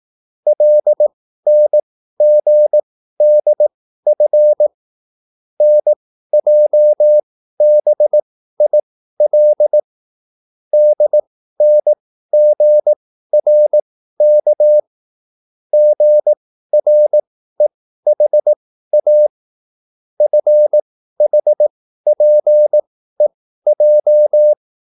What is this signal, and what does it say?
Morse code, heard as a keyed tone, about 18 wpm, spaced as for 13 wpm, 600 Hz. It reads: LNGDF NJBIL DNGRK GREHA FHPEJ